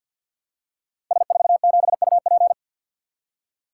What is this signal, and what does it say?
Morse code, heard as a keyed tone, about 53 wpm, 700 Hz. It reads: S47UP
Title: Code Groups - 5 Letter:Numbers - 50wpm